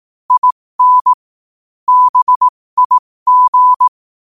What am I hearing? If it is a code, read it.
Morse code, heard as a keyed tone, about 18 wpm, spaced as for 15 wpm, 1000 Hz. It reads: IN BIG
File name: qotdfiles / 15wpm